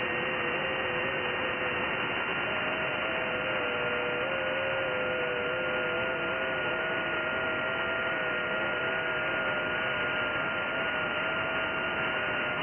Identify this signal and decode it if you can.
FT8.
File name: IC-7100